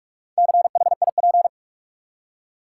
Morse code, heard as a keyed tone, about 46 wpm, 700 Hz. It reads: CHIP
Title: Four Letter Words - 45wpm